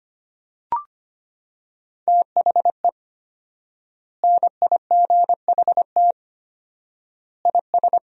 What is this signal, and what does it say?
Morse code, heard as a keyed tone, about 25 wpm, 700 Hz. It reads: THE NIGHT IS
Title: 4x Word-Spacing - Sentences from Top 500 Words - 25wpm